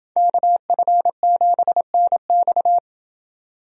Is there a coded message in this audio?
Yes: Morse code from 0.2 to 2.8 s, heard as a keyed tone, about 28 wpm, 700 Hz. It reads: KF7NX